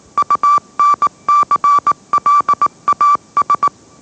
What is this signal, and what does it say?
Morse code, heard as a keyed tone, about 19 wpm, 1160 Hz. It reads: UNCLAS
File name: mrx / mrx081 20wpm